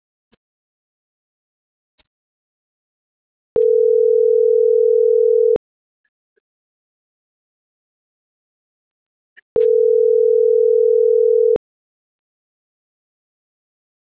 Ringback tone (North America), 440 + 480 Hz.